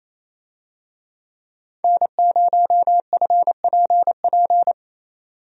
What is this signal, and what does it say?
Morse code, heard as a keyed tone, about 27 wpm, 700 Hz. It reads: N0FPP
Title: USA Call Signs - 5 Characters - 28wpm